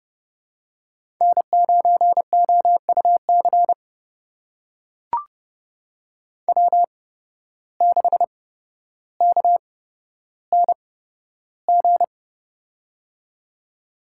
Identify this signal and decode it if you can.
Morse code, heard as a keyed tone, about 30 wpm, spaced as for 13 wpm, 700 Hz. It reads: N9OUC W6KNG
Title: USA Call Signs - 5 Characters - 8x Character Spacing - 30wpm